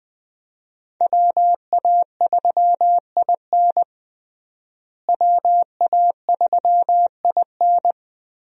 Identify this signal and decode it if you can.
Morse code, heard as a keyed tone, about 20 wpm, 700 Hz. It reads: WA3IN WA3IN
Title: USA Call Signs - 5 Characters - Repeated 2x - 20wpm